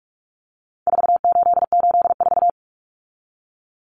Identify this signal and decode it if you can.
Morse code, heard as a keyed tone, about 44 wpm, 700 Hz. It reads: 4884
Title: Common Cut Numbers - 4-Digit 45wpm